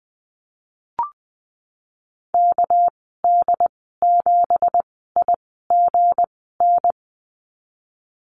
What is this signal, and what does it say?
Morse code, heard as a keyed tone, about 20 wpm, spaced as for 15 wpm, 700 Hz. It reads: KD7IGN
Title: 207 Lesson 33g - Letter j - Call Signs 20wpm